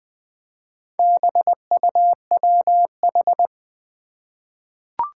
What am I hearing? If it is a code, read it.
Morse code, heard as a keyed tone, about 20 wpm, 700 Hz. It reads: BUWH
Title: Four Challenging Letter-Number Combinations 20wpm